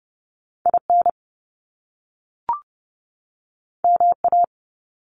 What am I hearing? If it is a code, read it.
Morse code, heard as a keyed tone, about 30 wpm, 700 Hz. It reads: IN MA
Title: US States 30wpm